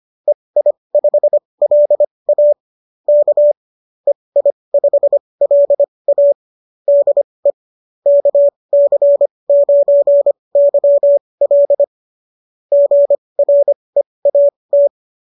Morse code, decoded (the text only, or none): EI5LA K EI5LA DE KC9YL GREAT